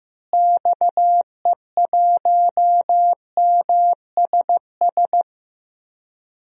Morse code, heard as a keyed tone, about 15 wpm, 700 Hz. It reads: XE1MSS
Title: DXCC-100 Call Signs - 6 Characters - Repeated 3x - 15wpm